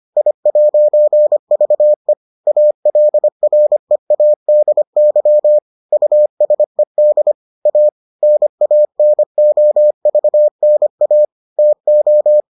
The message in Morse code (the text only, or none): I'VE ALREADY USED A NANOVNA TO